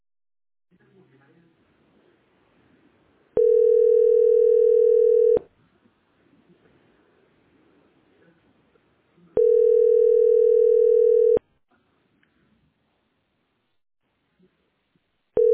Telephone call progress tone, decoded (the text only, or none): ringback tone